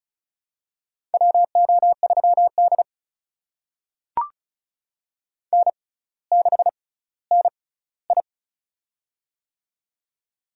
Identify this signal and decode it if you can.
Morse code, heard as a keyed tone, about 35 wpm, spaced as for 18 wpm, 700 Hz. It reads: WO3D N6NI
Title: USA Call Signs - 6x Character Spacing - 35wpm